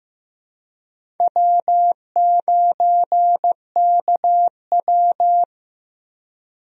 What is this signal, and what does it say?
Morse code, heard as a keyed tone, about 15 wpm, 700 Hz. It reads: W9KW
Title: USA Call Signs - 4 Characters - Repeated 3x - 15wpm